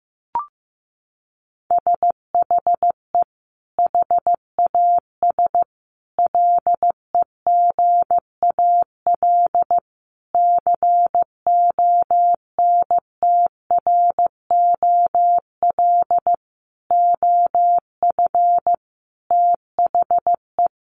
Morse code, heard as a keyed tone, about 15 wpm, 700 Hz. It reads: SHE HAS LEGAL CONTROL OF THE
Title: Sentences from Top 1200 Words 15wpm